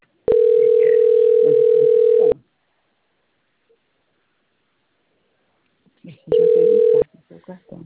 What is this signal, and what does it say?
Ringback tone (North America), 440 + 480 Hz.